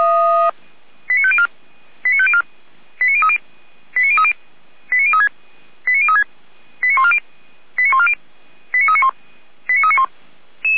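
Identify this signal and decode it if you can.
Five-tone selective call (ZVEI), heard as quick runs of five tones, tones of about 70 ms. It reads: 89504 89504 89030 89030 89036 89036 89140 89140 89391 89391
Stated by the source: ZVEI five-tone selective call received on SDR, channel 455, code 89042.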